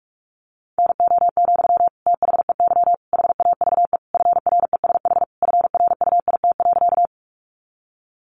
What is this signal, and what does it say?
Morse code, heard as a keyed tone, about 44 wpm, 700 Hz. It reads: NO, THEY HAVE FRESH FRUIT.
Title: Sentences from Top 1300 Words 45wpm